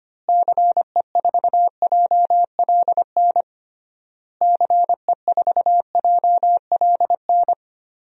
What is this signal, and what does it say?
Morse code, heard as a keyed tone, about 25 wpm, 700 Hz. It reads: CE4JLN CE4JLN
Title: DXCC-100 Call Signs - 6 Characters - Repeated 3x - 25wpm